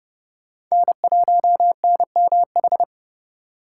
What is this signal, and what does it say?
Morse code, heard as a keyed tone, about 29 wpm, 700 Hz. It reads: N1NMH
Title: USA Call Signs - 5 Characters - 4x Character Spacing - 30wpm